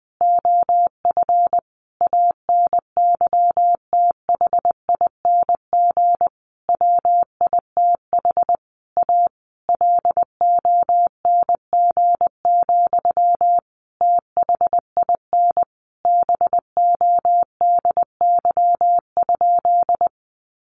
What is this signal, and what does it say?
Morse code, heard as a keyed tone, about 20 wpm, 700 Hz. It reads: OF ANYTHING WITH A LONG, THIN BODY?
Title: Sentences from Top 1200 Words 20wpm